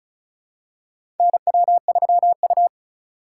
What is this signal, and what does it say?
Morse code, heard as a keyed tone, about 36 wpm, 700 Hz. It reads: NW3U
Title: USA Call Signs - 4 Characters - 3x Character Spacing - 35wpm